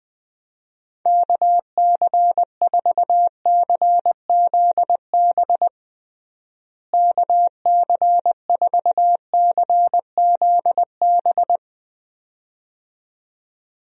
Morse code, heard as a keyed tone, about 20 wpm, 700 Hz. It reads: KC4CZB KC4CZB
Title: USA Call Signs - 6 Characters - Repeated 2x - 20wpm